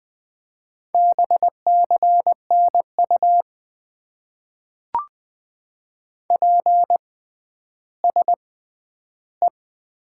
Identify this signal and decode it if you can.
Morse code, heard as a keyed tone, about 20 wpm, spaced as for 11 wpm, 700 Hz. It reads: BCNU PSE
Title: Top 90 QSO Elements - 6x Character Spacing - 20wpm